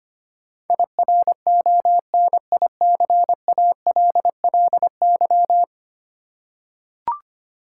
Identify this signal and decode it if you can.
Morse code, heard as a keyed tone, about 25 wpm, 700 Hz. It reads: IRONICALLY
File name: 10 Letter Words - First 3 Letters Given - 25wpm